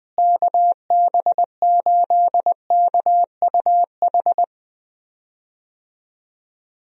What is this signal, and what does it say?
Morse code, heard as a keyed tone, about 21 wpm, 700 Hz. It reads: KB8KUH
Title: USA Call Signs - 6 Characters - Repeated 2x - 20wpm